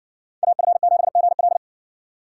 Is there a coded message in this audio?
Yes: Morse code from 0.4 to 1.6 s, heard as a keyed tone, about 60 wpm, 700 Hz. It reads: AF7GL